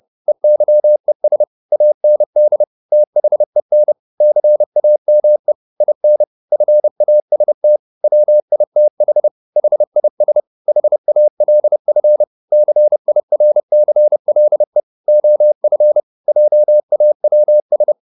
Morse code, heard as a keyed tone, about 30 wpm, 600 Hz. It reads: EYES AND THEN CAME IN FAST WITH HIS HALF CIRCLE OF JAWS